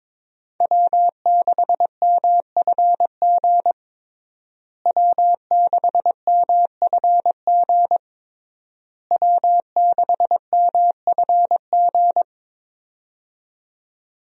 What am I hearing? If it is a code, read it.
Morse code, heard as a keyed tone, about 22 wpm, 700 Hz. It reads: W6MFG W6MFG W6MFG
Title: USA Call Signs - 5 Characters - Repeated 3x - 22wpm